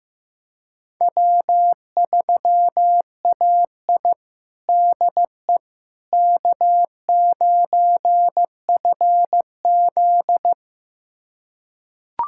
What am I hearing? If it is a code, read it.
Morse code, heard as a keyed tone, about 15 wpm, 700 Hz. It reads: W3AI DE K9FZ
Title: USA Callsign DE Callsign - 4 Characters - 15wpm